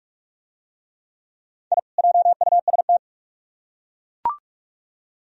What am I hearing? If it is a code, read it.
Morse code, heard as a keyed tone, about 42 wpm, 700 Hz. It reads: I JUST